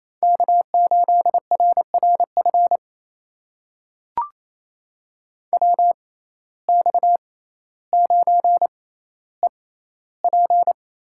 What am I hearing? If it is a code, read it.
Morse code, heard as a keyed tone, about 28 wpm, spaced as for 14 wpm, 700 Hz. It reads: K8RRF WX9EP
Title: USA Call Signs - 5 Characters - 6x Character Spacing - 28wpm